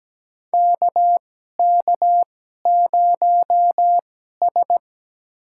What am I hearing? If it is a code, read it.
Morse code, heard as a keyed tone, about 17 wpm, spaced as for 12 wpm, 700 Hz. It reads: KK0S
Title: USA Call Signs - 4 Characters - 2x Character Spacing - 17wpm